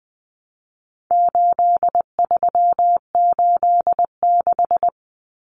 Morse code, heard as a keyed tone, about 20 wpm, 700 Hz. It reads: 8386
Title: Numbers - 4-Digit 20wpm